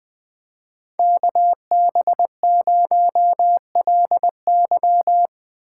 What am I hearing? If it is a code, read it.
Morse code, heard as a keyed tone, about 20 wpm, 700 Hz. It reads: KB0LY
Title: Speed Racing - USA 2x2 Call Signs 17wpm